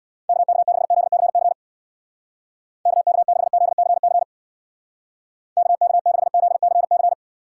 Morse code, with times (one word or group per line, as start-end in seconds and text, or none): 0.3-1.5 KK6ZCC
2.9-4.2 KK6ZCC
5.6-7.1 KK6ZCC